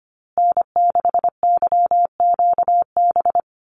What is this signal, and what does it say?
Morse code, heard as a keyed tone, about 25 wpm, 700 Hz. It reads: N6YQB
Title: USA Call Signs Repeated 2x 25wpm